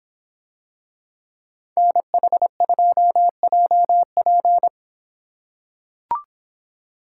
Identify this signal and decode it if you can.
Morse code, heard as a keyed tone, about 26 wpm, 700 Hz. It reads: NH2JP